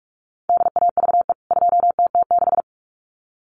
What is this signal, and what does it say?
Morse code, heard as a keyed tone, about 44 wpm, 700 Hz. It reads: DAVE 2TT6
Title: CWT Exchange 45wpm